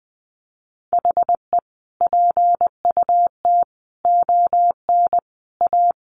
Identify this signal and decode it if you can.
Morse code, heard as a keyed tone, about 20 wpm, 700 Hz. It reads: HE PUT ON A
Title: Sentences from Top 1200 Words 20wpm